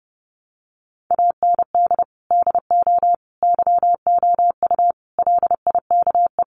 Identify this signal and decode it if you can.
Morse code, heard as a keyed tone, about 30 wpm, 700 Hz. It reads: AND DO YOU LIKE
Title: Sentences from Top 100 Words 30wpm